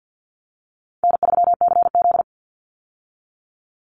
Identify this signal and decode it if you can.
Morse code, heard as a keyed tone, about 50 wpm, 700 Hz. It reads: N3CZ
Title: Easy Call Signs - 50wpm